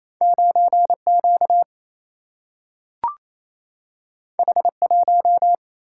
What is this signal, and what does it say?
Morse code, heard as a keyed tone, about 28 wpm, 700 Hz. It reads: 9 Q H 1
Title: Single-Letter Single-Number 28wpm